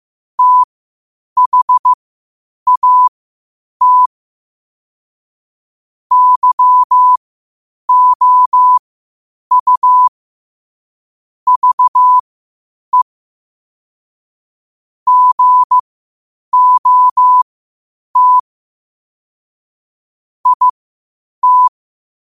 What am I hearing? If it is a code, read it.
Morse code, heard as a keyed tone, about 15 wpm, spaced as for 8 wpm, 1000 Hz. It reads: THAT YOU VE GOT IT
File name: qotdfiles / 8wpm